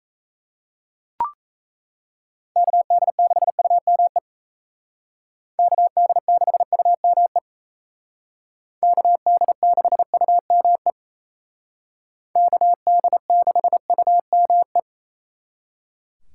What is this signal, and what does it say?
Morse code, heard as a keyed tone, about 36 wpm, 700 Hz. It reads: KD6UME KD6UME KD6UME KD6UME